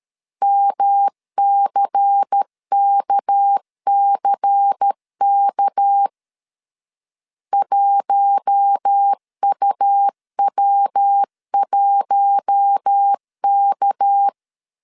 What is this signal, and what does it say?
Morse code, heard as a keyed tone, about 13 wpm, 800 Hz. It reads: MCKCK 1UW1K
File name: LEZIONI / 6 PKX0 / 6 PK 12wpm 2006